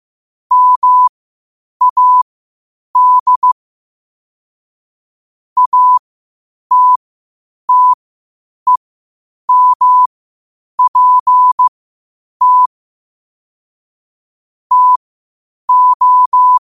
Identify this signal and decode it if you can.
Morse code, heard as a keyed tone, about 15 wpm, spaced as for 8 wpm, 1000 Hz. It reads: MAD ATTEMPT TO
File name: qotdfiles / 8wpm